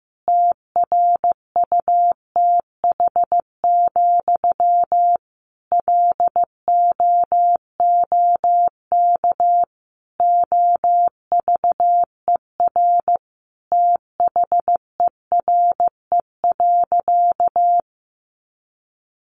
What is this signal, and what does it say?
Morse code, heard as a keyed tone, about 15 wpm, 700 Hz. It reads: TRUTH, LOOK OVER THERE.
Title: Sentences from Top 800 Words 15wpm